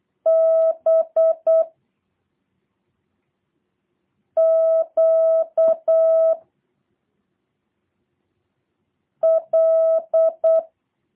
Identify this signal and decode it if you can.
Morse code, heard as a keyed tone, about 8 wpm, 650 Hz. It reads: B Q L
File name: jh8cbh / 2023cw / review1-6